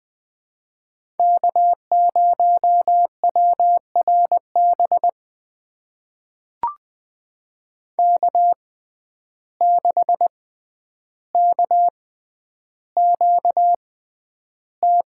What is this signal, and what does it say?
Morse code, heard as a keyed tone, about 20 wpm, spaced as for 10 wpm, 700 Hz. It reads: K0WRB K6KQT
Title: USA Call Signs - 5 Characters - 6x Character Spacing - 20wpm